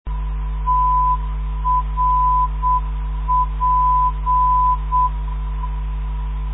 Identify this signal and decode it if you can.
Morse code, heard as a keyed tone, about 7 wpm, 1000 Hz. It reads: TRP
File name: ndb countryes / zvoki mp3 / trp 317.5khz mp3